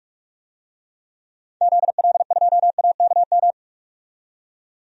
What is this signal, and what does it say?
Morse code, heard as a keyed tone, about 42 wpm, 700 Hz. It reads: ZL2AKM
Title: DXCC-100 Call Signs - 6 Characters - 45wpm